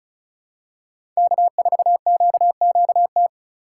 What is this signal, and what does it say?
Morse code, heard as a keyed tone, about 34 wpm, 700 Hz. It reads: K4QQT
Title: USA Call Signs - 5 Characters - 2x Character Spacing - 35wpm